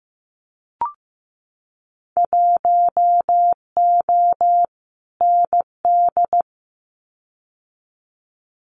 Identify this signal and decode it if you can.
Morse code, heard as a keyed tone, about 15 wpm, 700 Hz. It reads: 1O ND
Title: Winter Field Day Exchanges 15wpm